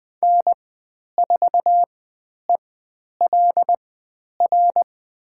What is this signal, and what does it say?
Morse code, heard as a keyed tone, about 20 wpm, spaced as for 10 wpm, 700 Hz. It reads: N4ELR
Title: 095 Lesson 16g - Letter c - Call Signs 20wpm